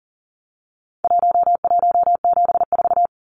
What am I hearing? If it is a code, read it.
Morse code, heard as a keyed tone, about 40 wpm, 700 Hz. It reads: 1174
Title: Numbers - Contest Style - 4-Digit 40wpm